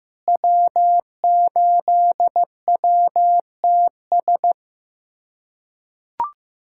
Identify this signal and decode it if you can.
Morse code, heard as a keyed tone, about 15 wpm, 700 Hz. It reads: W8WTS
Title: USA Callsign DE Callsign - 5 Characters - 15wpm